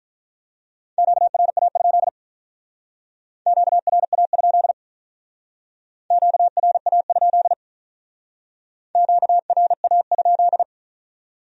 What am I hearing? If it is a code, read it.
Morse code, heard as a keyed tone, about 44 wpm, 700 Hz. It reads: QRA? QRA? QRA? QRA?